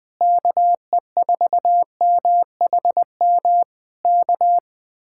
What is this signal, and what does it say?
Morse code, heard as a keyed tone, about 20 wpm, 700 Hz. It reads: KE4MHM K
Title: Speed-Racing Common QSO Phrases - 17wpm